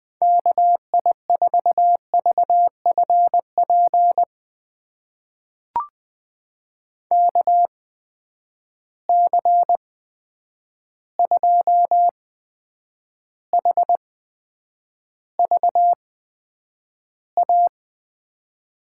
Morse code, heard as a keyed tone, about 20 wpm, spaced as for 9 wpm, 700 Hz. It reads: KI4VFP KC2HVA
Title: USA Call Signs - 6 Characters - 8x Character Spacing - 20wpm